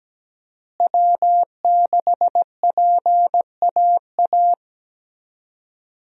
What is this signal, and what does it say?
Morse code, heard as a keyed tone, about 17 wpm, 700 Hz. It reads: W6PAA
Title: USA Call Signs - 5 Characters - Repeated 2x - 17wpm